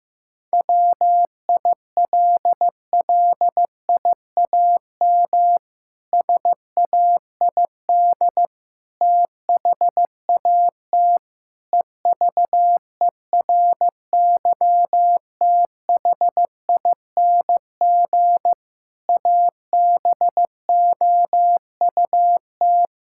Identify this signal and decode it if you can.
Morse code, heard as a keyed tone, about 15 wpm, 700 Hz. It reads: WILLIAM SAID THAT EVERYTHING ABOUT
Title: Pangram Challenge v2 - 15wpm